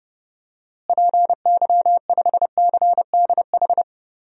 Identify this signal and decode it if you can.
Morse code, heard as a keyed tone, about 32 wpm, 700 Hz. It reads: PY5CDH
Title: DXCC-100 Call Signs - 6 Characters - Repeated 3x - 30wpm